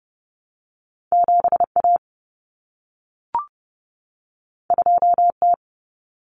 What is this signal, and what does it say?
Morse code, heard as a keyed tone, about 30 wpm, 700 Hz. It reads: 7A 2T
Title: Common Cut Numbers - 2-Digit 30wpm